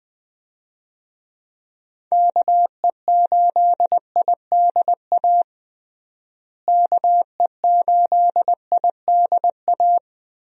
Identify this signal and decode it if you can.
Morse code, heard as a keyed tone, about 20 wpm, 700 Hz. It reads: KE8IDA KE8IDA
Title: USA Call Signs - 6 Characters - Repeated 3x - 20wpm